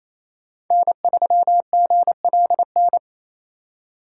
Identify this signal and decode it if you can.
Morse code, heard as a keyed tone, about 35 wpm, 700 Hz. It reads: N3GLN